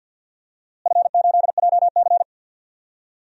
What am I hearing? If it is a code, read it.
Morse code, heard as a keyed tone, about 52 wpm, 700 Hz. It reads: F8JC
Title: DXCC-100 Call Signs - 4 Characters - Repeated 2x - 50wpm